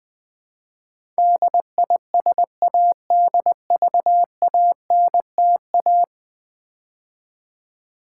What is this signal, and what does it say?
Morse code, heard as a keyed tone, about 19 wpm, 700 Hz. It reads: DISADVANTA
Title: Build A Word - 5 Letters At A Time - 12-Letter Words - 20wpm